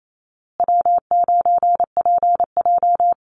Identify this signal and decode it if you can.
Morse code, heard as a keyed tone, about 28 wpm, 700 Hz. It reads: W9PJ
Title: Call Signs Repeated 3x - 28wpm